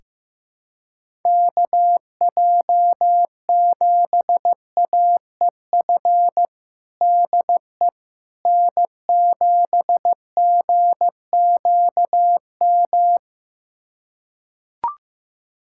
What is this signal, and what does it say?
Morse code, heard as a keyed tone, about 15 wpm, 700 Hz. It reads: KJ7AEF DE N7GQM
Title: USA Callsign DE Callsign 15wpm